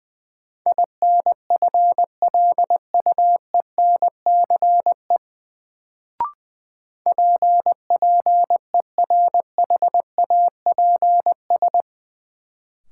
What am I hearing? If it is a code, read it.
Morse code, heard as a keyed tone, about 21 wpm, 700 Hz. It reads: INFLUENCE PPERHAPS
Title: Sentences from Top 1100 Words - New Word Drill Alt - 20wpm